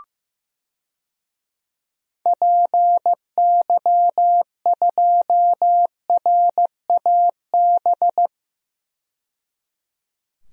Morse code, heard as a keyed tone, about 15 wpm, 700 Hz. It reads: PY2RAB
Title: DXCC-100 Call Signs - 6 Characters - 15wpm